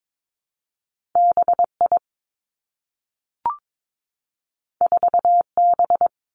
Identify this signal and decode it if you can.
Morse code, heard as a keyed tone, about 22 wpm, 700 Hz. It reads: B I 4 B
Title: Two Challenging Letter-Number Combinations 22wpm